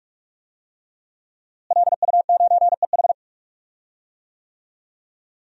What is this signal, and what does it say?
Morse code, heard as a keyed tone, about 41 wpm, 700 Hz. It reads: LU9EH
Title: DXCC-100 Call Signs - 5 Characters - Repeated 3x - 45wpm